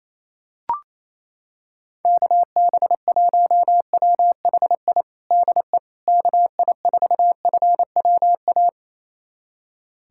Morse code, heard as a keyed tone, about 28 wpm, 700 Hz. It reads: KB1WHI DE KI4FWA
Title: USA Callsign DE Callsign - 6 Characters - 28wpm